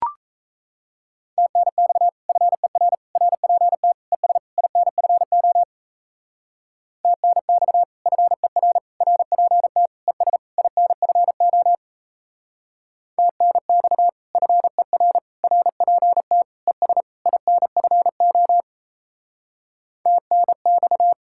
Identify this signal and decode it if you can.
Morse code, heard as a keyed tone, about 35 wpm, 700 Hz. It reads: TNX FER RPT ES INFO TNX FER RPT ES INFO TNX FER RPT ES INFO TNX